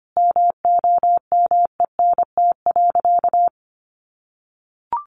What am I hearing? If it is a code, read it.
Morse code, heard as a keyed tone, about 25 wpm, 700 Hz. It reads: MOMENT.
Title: Sentences from Top 1000 Words 25wpm